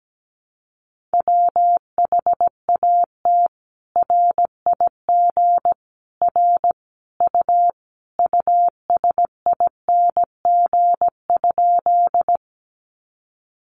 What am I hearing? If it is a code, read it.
Morse code, heard as a keyed tone, about 17 wpm, 700 Hz. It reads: WHAT RIG R U USING?
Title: Common QSO Phrases 17wpm